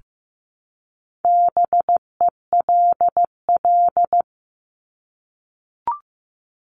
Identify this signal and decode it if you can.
Morse code, heard as a keyed tone, about 15 wpm, 700 Hz. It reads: BELL
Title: Top 1751-2000 Words 15wpm